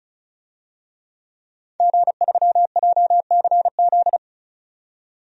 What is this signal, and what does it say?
Morse code, heard as a keyed tone, about 34 wpm, 700 Hz. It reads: G3JCZ
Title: DXCC-100 Call Signs - 5 Characters - 35wpm